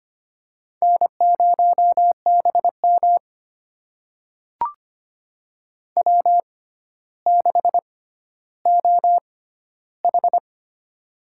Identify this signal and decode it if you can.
Morse code, heard as a keyed tone, about 25 wpm, spaced as for 13 wpm, 700 Hz. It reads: N0BM W6OH
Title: USA Call Signs - 4 Characters - 6x Character Spacing - 25wpm